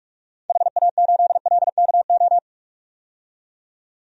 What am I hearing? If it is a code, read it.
Morse code, heard as a keyed tone, about 41 wpm, 700 Hz. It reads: HA8LKO
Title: DXCC-100 Call Signs - 6 Characters - 45wpm